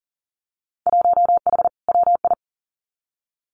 Morse code, heard as a keyed tone, about 40 wpm, 700 Hz. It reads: 1H WI
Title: Winter Field Day Exchanges 40wpm